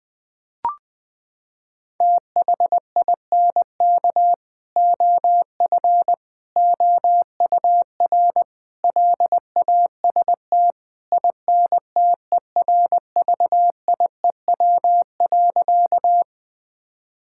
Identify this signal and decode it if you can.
Morse code, heard as a keyed tone, about 20 wpm, 700 Hz. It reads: THINK OF OUR LAST INTERVIEW.
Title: Sentences from Top 1600 Words 20wpm